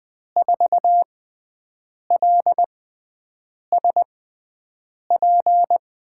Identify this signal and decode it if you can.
Morse code, heard as a keyed tone, about 21 wpm, spaced as for 7 wpm, 700 Hz. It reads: 4LSP
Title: USA Call Signs - 5 Characters - 6x Character Spacing - 20wpm